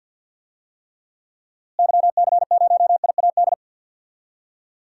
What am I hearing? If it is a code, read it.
Morse code, heard as a keyed tone, about 47 wpm, 700 Hz. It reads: YC0IAD
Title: DXCC-100 Call Signs - 6 Characters - Repeated 2x - 50wpm